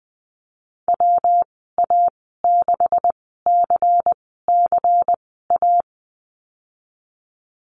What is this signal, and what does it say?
Morse code, heard as a keyed tone, about 19 wpm, spaced as for 14 wpm, 700 Hz. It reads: WA6CCA
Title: 140 Lesson 23g - Letter f - Call Signs 20wpm